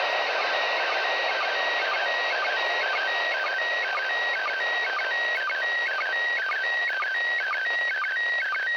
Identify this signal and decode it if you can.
SSTV (slow-scan television), PD 120.
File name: 08-ARISS-SSTV-Expedition-63-ARISS-Series-16-Satellites